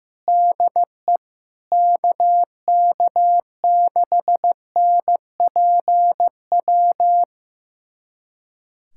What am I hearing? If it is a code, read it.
Morse code, heard as a keyed tone, about 15 wpm, 700 Hz. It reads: DE KK6NPW
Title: USA Callsign DE Callsign - 6 Characters - 15wpm